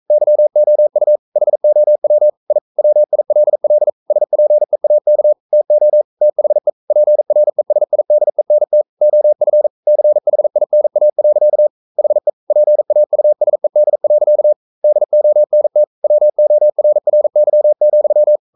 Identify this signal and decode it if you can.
Morse code, heard as a keyed tone, about 42 wpm, 600 Hz. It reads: YOU HOW I WILL SPEAK TO THE PRESIDENT OF CHINA. HE PAUSED. DONT WORRY,
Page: Romanen Never af Ken Follet på engelsk.